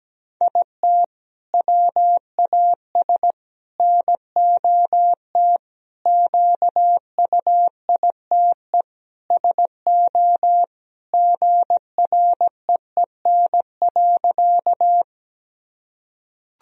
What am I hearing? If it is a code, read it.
Morse code, heard as a keyed tone, about 17 wpm, 700 Hz. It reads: IT WAS NOT QUITE SO GREEN.
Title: Sentences from Top 1000 Words - Review Part 2 - 17wpm